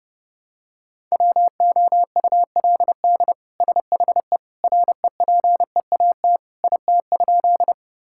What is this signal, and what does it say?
Morse code, heard as a keyed tone, about 30 wpm, 700 Hz. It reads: WOULD SHE REPEAT IT?
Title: Sentences from Top 1600 Words 30wpm